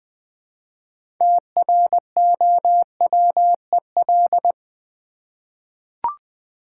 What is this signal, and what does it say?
Morse code, heard as a keyed tone, about 20 wpm, 700 Hz. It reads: TROWEL
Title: Easy Words with Hints - 20wpm